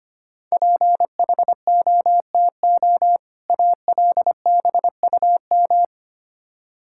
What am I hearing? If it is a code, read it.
Morse code, heard as a keyed tone, about 25 wpm, 700 Hz. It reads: PHOTO ALBUM
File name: practice / 25wpm / Difficult Words with Hints - 25wpm